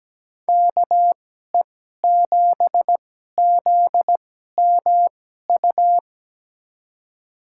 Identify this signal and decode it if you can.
Morse code, heard as a keyed tone, about 17 wpm, spaced as for 12 wpm, 700 Hz. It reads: KE7ZMU
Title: USA Call Signs - 6 Characters - 2x Character Spacing - 17wpm